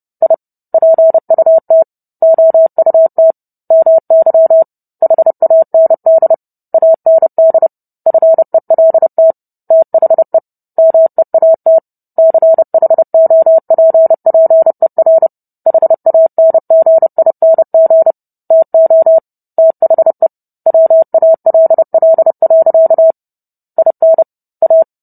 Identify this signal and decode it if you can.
Morse code, heard as a keyed tone, about 30 wpm, 650 Hz. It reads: I PUT OUT MY HAND AND FELT THE MEAT CHOPPER HANGING TO THE WALL. IN A